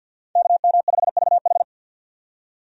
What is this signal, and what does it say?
Morse code, heard as a keyed tone, about 50 wpm, 700 Hz. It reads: KM5VH
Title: USA Call Signs - 5 Characters - 50wpm